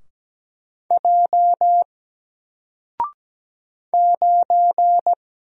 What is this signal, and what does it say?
Morse code, heard as a keyed tone, about 17 wpm, 700 Hz. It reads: J 9
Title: Single Letter-Number Alt 17wpm